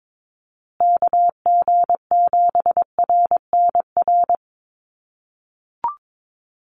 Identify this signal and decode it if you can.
Morse code, heard as a keyed tone, about 22 wpm, 700 Hz. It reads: KG7RNR
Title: USA Call Signs Repeated 2x 22wpm